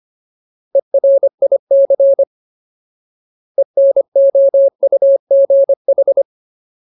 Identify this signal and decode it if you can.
Morse code, heard as a keyed tone, about 25 wpm, 550 Hz. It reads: ERIC ENOUGH
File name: Ultimate Word List / 25WPM Ultimate